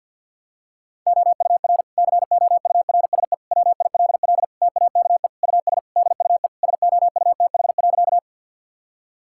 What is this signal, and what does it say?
Morse code, heard as a keyed tone, about 52 wpm, 700 Hz. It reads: OUR COURSE WILL TAKE US DUE SOUTH.
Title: Sentences from Top 1000 Words - Review Part 6 - 50wpm